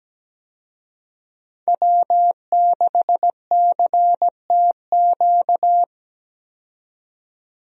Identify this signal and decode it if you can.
Morse code, heard as a keyed tone, about 17 wpm, 700 Hz. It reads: W6CTQ
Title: USA Call Signs - 5 Characters - 17wpm